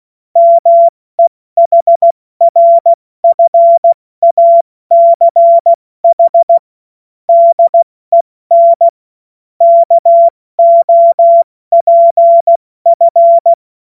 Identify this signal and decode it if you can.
Morse code, heard as a keyed tone, about 16 wpm, spaced as for 14 wpm, 680 Hz. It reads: MEHRFACH DEN KOPF
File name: Morsekurs / Witze / Witz 23. Dezember 2025 70 F80